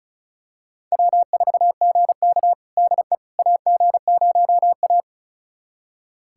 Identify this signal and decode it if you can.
Morse code, heard as a keyed tone, about 36 wpm, 700 Hz. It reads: W4GK DE AG0A
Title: USA Callsign DE Callsign - 4 Characters - 35wpm